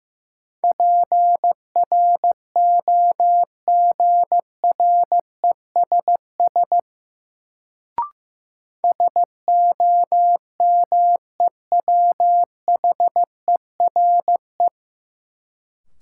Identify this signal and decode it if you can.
Morse code, heard as a keyed tone, about 15 wpm, 700 Hz. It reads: PROGRESS SOMEWHERE
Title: Sentences from Top 1400 Words - New Word Drill - 15wpm